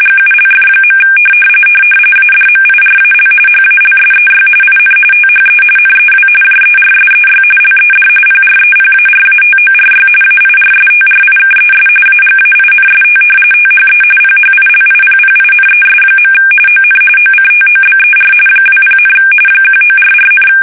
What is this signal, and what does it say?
RTTY, about 45.45 baud. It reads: GULF. ADMIRAL HALSEY IN THE NEW JERSEY LED PREPARATORY AIRSTRIKES AGAINST THE PHILIPPINES. ON OCTOBER 20, 1944,
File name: fsk850 / battleshipnj